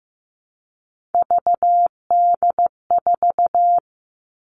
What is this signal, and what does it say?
Morse code, heard as a keyed tone, about 15 wpm, 700 Hz. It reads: VD4
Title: Three Challenging Letter-Number Combinations 15wpm